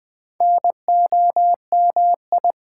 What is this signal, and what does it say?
Morse code, heard as a keyed tone, about 20 wpm, 700 Hz. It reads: NOMI
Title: Build A Word - 4 Letters At A Time - 8-Letter Words - 20wpm